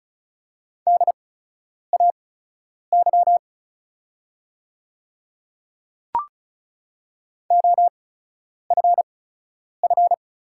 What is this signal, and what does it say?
Morse code, heard as a keyed tone, about 36 wpm, spaced as for 10 wpm, 700 Hz. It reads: DAY OFF
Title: 8x Character Spacing - Top 300 Words - 35wpm